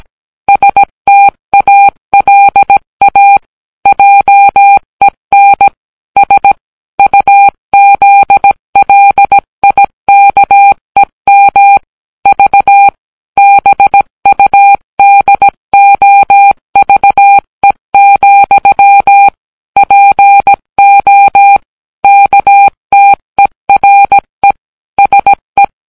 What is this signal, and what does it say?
Morse code, heard as a keyed tone, about 17 wpm, 800 Hz. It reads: STALA JEN S UZLIKEM V BUDOVE, PO KTERE SE